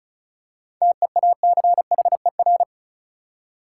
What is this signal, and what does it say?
Morse code, heard as a keyed tone, about 36 wpm, 700 Hz. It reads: TEACHER
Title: Seven Letter Words - 35wpm